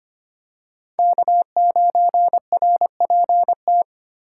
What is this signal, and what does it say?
Morse code, heard as a keyed tone, about 25 wpm, 700 Hz. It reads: K9RPT
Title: USA Call Signs - 2x Character Spacing - 25wpm